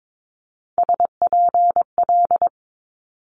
Morse code, heard as a keyed tone, about 22 wpm, 700 Hz. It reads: SPL
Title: Multi-letter Phonemes Quick Fire 22wpm